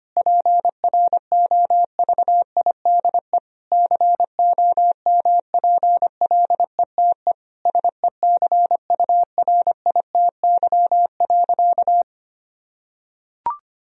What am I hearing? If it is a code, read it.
Morse code, heard as a keyed tone, about 25 wpm, 700 Hz. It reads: PROVIDE COMPLETE SECURITY.
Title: Sentences from Top 1000 Words - Review Part 4 - 25wpm